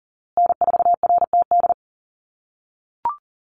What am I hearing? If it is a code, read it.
Morse code, heard as a keyed tone, about 40 wpm, 700 Hz. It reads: N4RTD
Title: USA Call Signs Repeated 3x 40wpm